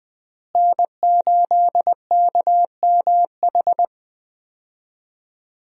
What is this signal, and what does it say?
Morse code, heard as a keyed tone, about 20 wpm, 700 Hz. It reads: N8KMH
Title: USA Call Signs - 5 Characters - Repeated 3x - 20wpm